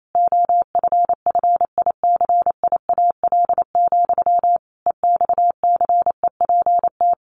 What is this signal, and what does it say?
Morse code, heard as a keyed tone, about 28 wpm, 700 Hz. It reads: OFFICIAL, EXCEPT
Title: A Sound Of Thunder 28wpm